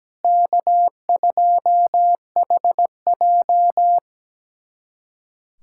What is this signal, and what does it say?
Morse code, heard as a keyed tone, about 17 wpm, 700 Hz. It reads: K2HJ
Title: USA Callsign DE Callsign - 4 Characters - 17wpm